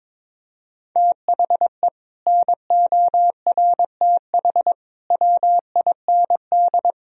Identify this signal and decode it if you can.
Morse code, heard as a keyed tone, about 21 wpm, 700 Hz. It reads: THE NORTH WIND
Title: Sentences from Top 1000 Words - Review Part 2 - 22wpm